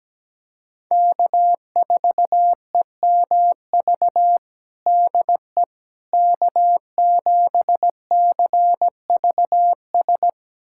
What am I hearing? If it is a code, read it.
Morse code, heard as a keyed tone, about 17 wpm, 700 Hz. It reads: K4EMV DE K7CVS
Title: USA Callsign DE Callsign - 5 Characters - 17wpm